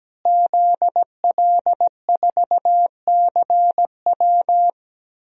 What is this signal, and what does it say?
Morse code, heard as a keyed tone, about 17 wpm, 700 Hz. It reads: ZL4CW
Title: DXCC-100 Call Signs - 5 Characters - Repeated 2x - 17wpm